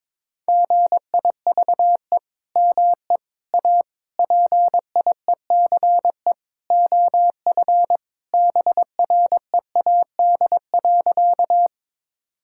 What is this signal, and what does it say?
Morse code, heard as a keyed tone, about 22 wpm, 700 Hz. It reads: GIVE ME A PIECE OF BREAD.
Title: Sentences from Top 1600 Words 22wpm